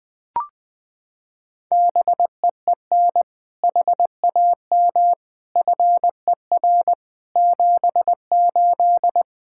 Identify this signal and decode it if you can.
Morse code, heard as a keyed tone, about 20 wpm, 700 Hz. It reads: BEEN HAM FER 78
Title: Practice QSO - Exchange 3 - More Info - 20wpm